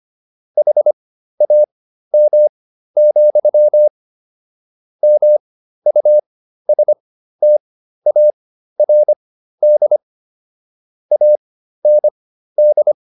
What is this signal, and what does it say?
Morse code, heard as a keyed tone, about 25 wpm, spaced as for 13 wpm, 600 Hz. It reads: HAM, MUSTARD AND